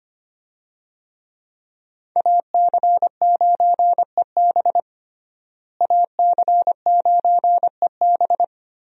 Morse code, heard as a keyed tone, about 25 wpm, 700 Hz. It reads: AC9EB AC9EB
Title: USA Call Signs - 5 Characters - Repeated 3x - 25wpm